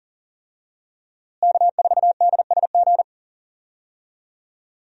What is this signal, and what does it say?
Morse code, heard as a keyed tone, about 40 wpm, 700 Hz. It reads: K4DSG
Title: USA Call Signs - 5 Characters - Repeated 2x - 40wpm